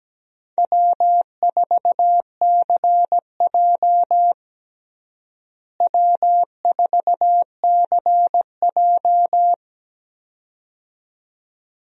Morse code, heard as a keyed tone, about 17 wpm, 700 Hz. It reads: W4CJ W4CJ
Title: USA Call Signs - 4 Characters - Repeated 2x - 17wpm